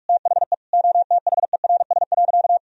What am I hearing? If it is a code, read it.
Morse code, heard as a keyed tone, about 41 wpm, 700 Hz. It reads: THE OTHERS.
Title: Sentences from Top 1000 Words - Review Part 2 - 45wpm